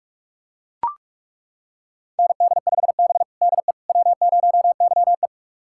Morse code, heard as a keyed tone, about 41 wpm, 700 Hz. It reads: ND5B DE W0YE
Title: USA Callsign DE Callsign - 4 Characters - 45wpm